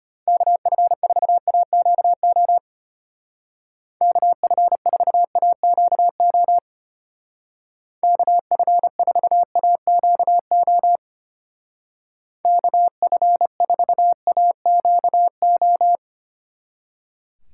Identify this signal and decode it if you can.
Morse code, heard as a keyed tone, about 31 wpm, 700 Hz. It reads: KF4AQO KF4AQO KF4AQO KF4AQO